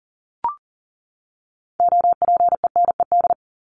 Morse code, heard as a keyed tone, about 40 wpm, 700 Hz. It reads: OPENED
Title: ED Suffix - Top 100 Words with Base Words 40wpm